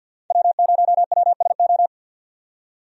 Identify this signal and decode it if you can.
Morse code, heard as a keyed tone, about 51 wpm, 700 Hz. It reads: W0WSO
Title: USA Call Signs - 5 Characters - 50wpm